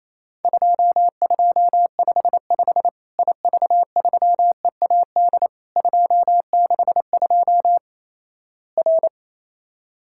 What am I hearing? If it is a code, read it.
Morse code, heard as a keyed tone, about 28 wpm, 700 Hz. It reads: 2255 IV3EAD 262 R
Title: Worked All Europe DX Contest - QTC Exchange - 28wpm